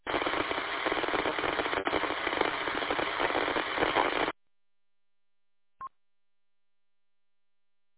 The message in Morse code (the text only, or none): none